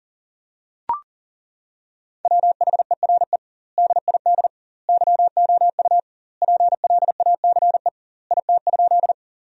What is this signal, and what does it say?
Morse code, heard as a keyed tone, about 40 wpm, 700 Hz. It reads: WHERE DID YOU PLACE IT?
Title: 2x Word-Spacing - Sentences from Top 200 Words - 40wpm